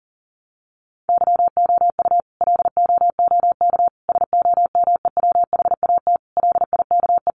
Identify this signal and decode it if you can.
Morse code, heard as a keyed tone, about 40 wpm, 700 Hz. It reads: YOU LOOK SOMEWHAT LIKE
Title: Sentences from Top 1300 Words 40wpm